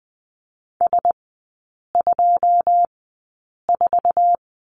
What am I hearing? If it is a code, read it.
Morse code, heard as a keyed tone, about 20 wpm, spaced as for 8 wpm, 700 Hz. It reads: S24
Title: 081 Lesson 14e - Number 2 - Three Characters 20wpm